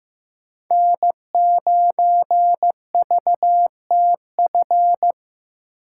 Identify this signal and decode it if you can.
Morse code, heard as a keyed tone, about 18 wpm, 700 Hz. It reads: N9VTF
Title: Speed Racing - USA 1x3 Call Signs 15wpm